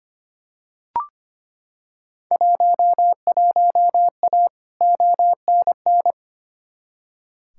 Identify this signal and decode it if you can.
Morse code, heard as a keyed tone, about 25 wpm, 700 Hz. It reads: 11A ONN
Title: Field Day Exchanges Encore - 25wpm